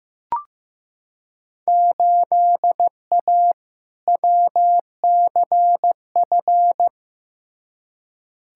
Morse code, heard as a keyed tone, about 15 wpm, 700 Hz. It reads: 8A WCF
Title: Field Day Exchanges Encore - 15wpm